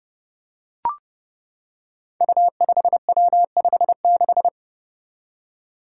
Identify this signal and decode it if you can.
Morse code, heard as a keyed tone, about 31 wpm, 700 Hz. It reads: U5W56
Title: Five Challenging Letter-Number Combinations 30wpm